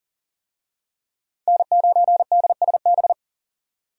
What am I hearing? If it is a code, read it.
Morse code, heard as a keyed tone, about 40 wpm, 700 Hz. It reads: N9DSB
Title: USA Call Signs - 5 Characters - 40wpm